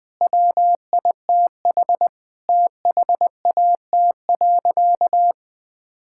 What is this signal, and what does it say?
Morse code, heard as a keyed tone, about 20 wpm, 700 Hz. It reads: WITH THAT.
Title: Sentences from Top 1000 Words - Review Part 5 - 20wpm